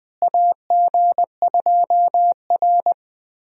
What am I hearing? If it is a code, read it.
Morse code, heard as a keyed tone, about 20 wpm, 700 Hz. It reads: AG2R
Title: USA Call Signs - 4 Characters - Repeated 3x - 20wpm